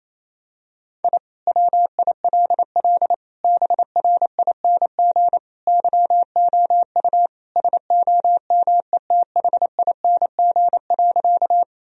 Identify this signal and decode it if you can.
Morse code, heard as a keyed tone, about 28 wpm, 700 Hz. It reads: I WILL BRING YOU SOMETHING.
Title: Sentences from Top 1000 Words - Review Part 3 - 28wpm